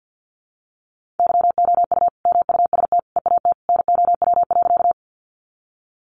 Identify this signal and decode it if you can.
Morse code, heard as a keyed tone, about 50 wpm, 700 Hz. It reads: YOU MUST EAT NOW.
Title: Sentences from Top 300 Words 50wpm